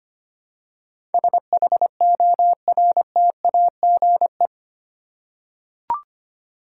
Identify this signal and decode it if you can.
Morse code, heard as a keyed tone, about 25 wpm, 700 Hz. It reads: SHORTAGE
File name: Eight Letter Words - 25wpm